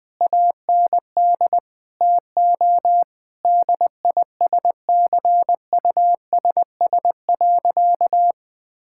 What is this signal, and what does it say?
Morse code, heard as a keyed tone, about 20 wpm, 700 Hz. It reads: AND TO DISCUSS.
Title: Sentences with 2-Word Phrases - 20wpm